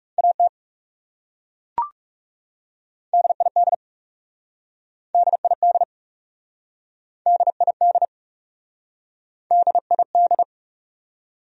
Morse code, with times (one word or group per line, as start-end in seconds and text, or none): 0.2-0.5 AT
3.1-3.7 DID
5.1-5.8 DID
7.3-8.1 DID
9.5-10.4 DID